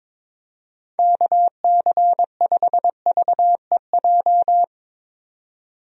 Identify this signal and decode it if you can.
Morse code, heard as a keyed tone, about 21 wpm, 700 Hz. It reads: KC5VEJ
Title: USA Call Signs - 6 Characters - 22wpm